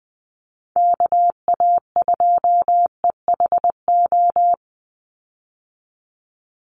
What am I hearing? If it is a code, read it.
Morse code, heard as a keyed tone, about 20 wpm, 700 Hz. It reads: KA2EHO
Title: USA Call Signs 20wpm